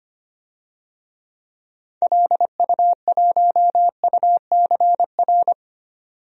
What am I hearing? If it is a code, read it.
Morse code, heard as a keyed tone, about 25 wpm, 700 Hz. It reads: LU1UCR
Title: DXCC-100 Call Signs - 6 Characters - 25wpm